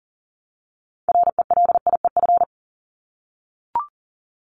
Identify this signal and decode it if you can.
Morse code, heard as a keyed tone, about 40 wpm, 700 Hz. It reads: RELIEF
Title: Top 1501-1750 Words 40wpm